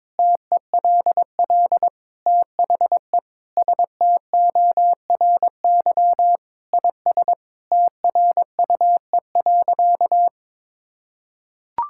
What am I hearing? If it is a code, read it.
Morse code, heard as a keyed tone, about 22 wpm, 700 Hz. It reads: TELL THE STORY IS TRUE.
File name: Sentences with 7-Word Phrases - 22wpm